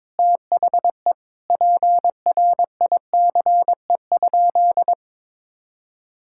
Morse code, heard as a keyed tone, about 22 wpm, 700 Hz. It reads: THE PRICE?
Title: Sentences from Top 1600 Words 22wpm